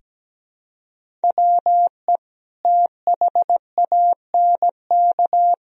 Morse code, heard as a keyed tone, about 17 wpm, 700 Hz. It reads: WE THANK
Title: Sentences from Top 1000 Words - Review Part 1 - 17wpm